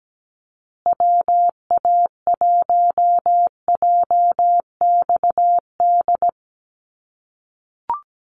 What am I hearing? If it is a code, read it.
Morse code, heard as a keyed tone, about 17 wpm, 700 Hz. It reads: WA1JXD
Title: USA Call Signs Repeated 2x 17wpm